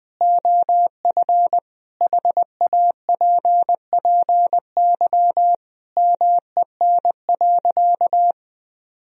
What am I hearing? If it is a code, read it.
Morse code, heard as a keyed tone, about 20 wpm, 700 Hz. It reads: OF HAPPY MEN.
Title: Sentences from Top 1000 Words - Review Part 1 - 20wpm